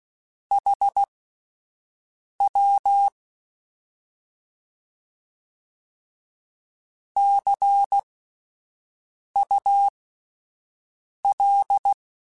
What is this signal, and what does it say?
Morse code, heard as a keyed tone, about 16 wpm, spaced as for 5 wpm, 800 Hz. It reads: HW CUL
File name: Misc / Misc-mp3s / Words @5